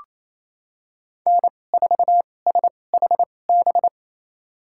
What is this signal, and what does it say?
Morse code, heard as a keyed tone, about 28 wpm, spaced as for 20 wpm, 700 Hz. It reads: N4SHB
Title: USA Call Signs - 2x Character Spacing - 28wpm